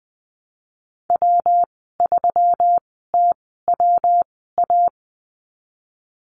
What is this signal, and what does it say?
Morse code, heard as a keyed tone, about 21 wpm, spaced as for 15 wpm, 700 Hz. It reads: W3TWA
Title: 122 Lesson 20g - Number 3 - Call Signs 20wpm